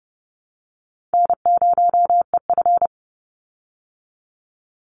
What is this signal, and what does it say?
Morse code, heard as a keyed tone, about 30 wpm, 700 Hz. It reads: N0EF
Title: Call Signs Repeated 2x - 30wpm